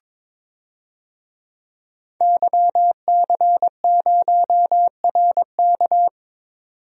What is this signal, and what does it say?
Morse code, heard as a keyed tone, about 21 wpm, 700 Hz. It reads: YC0RK
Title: DXCC-100 Call Signs - 5 Characters - 22wpm